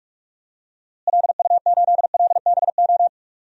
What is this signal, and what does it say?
Morse code, heard as a keyed tone, about 46 wpm, 700 Hz. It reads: LU8LBO
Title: DXCC-100 Call Signs - 6 Characters - Repeated 2x - 45wpm